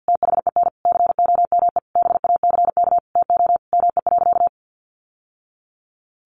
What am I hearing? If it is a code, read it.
Morse code, heard as a keyed tone, about 50 wpm, 700 Hz. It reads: THEN COME BACK TO ME.
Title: Sentences from Top 600 Words 50wpm